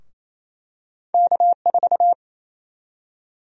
Morse code, heard as a keyed tone, about 28 wpm, 700 Hz. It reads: K4
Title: USA Call Sign Prefixes - Repeated 2x - 28wpm